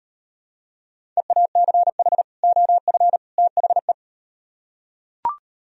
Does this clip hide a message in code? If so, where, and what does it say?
Yes: Morse code from 1.2 to 3.9 s, heard as a keyed tone, about 38 wpm, 700 Hz. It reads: EACH OF THE